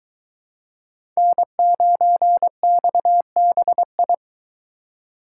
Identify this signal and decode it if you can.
Morse code, heard as a keyed tone, about 22 wpm, 700 Hz. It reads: N9XBI